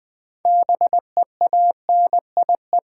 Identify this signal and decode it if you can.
Morse code, heard as a keyed tone, about 21 wpm, 700 Hz. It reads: BEANIE
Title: Difficult Words with Hints - 20wpm